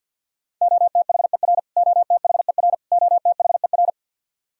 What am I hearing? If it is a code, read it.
Morse code, heard as a keyed tone, about 54 wpm, 700 Hz. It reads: OTHER OTHER OTHER
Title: Top 100 Words Repeated 50wpm